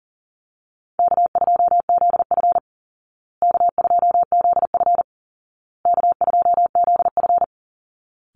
Morse code, heard as a keyed tone, about 40 wpm, 700 Hz. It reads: K2ZF K2ZF K2ZF
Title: Easy Call Signs Repeated 3x - 40wpm